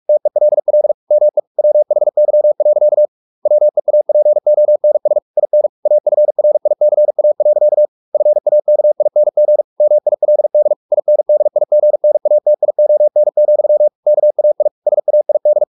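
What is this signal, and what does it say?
Morse code, heard as a keyed tone, about 44 wpm, 600 Hz. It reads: TELL ME WHY. WEAPONS IN AFRICA. FAKING MILD INDIGNATION, KAI SAID
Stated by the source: Morse practice at 45 wpm, the novel Never by Ken Follett in English, from CW med Gnister.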